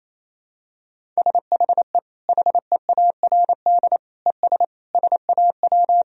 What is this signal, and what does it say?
Morse code, heard as a keyed tone, about 28 wpm, 700 Hz. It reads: SHE HEARD ES SAW
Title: Sentences from Top 500 Words and Abbreviations 28wpm